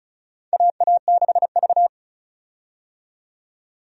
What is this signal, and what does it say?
Morse code, heard as a keyed tone, about 36 wpm, 700 Hz. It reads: AA6V
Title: USA Call Signs - 4 Characters - 35wpm